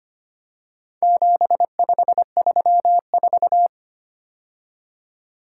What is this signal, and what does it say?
Morse code, heard as a keyed tone, about 25 wpm, 700 Hz. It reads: 7534
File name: practice / 25wpm / Four Letter-Number Combinations - 25wpm